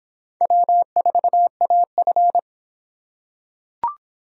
Morse code, heard as a keyed tone, about 26 wpm, 700 Hz. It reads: W4AF